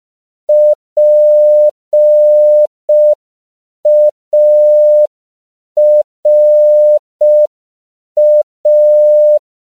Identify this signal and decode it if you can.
Morse code, heard as a keyed tone, about 5 wpm, 600 Hz. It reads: PARA
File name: CW / 5PPM